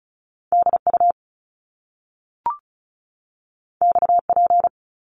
Morse code, heard as a keyed tone, about 36 wpm, 700 Hz. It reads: DU XP
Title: Top Two Letter Combinations in Words 35wpm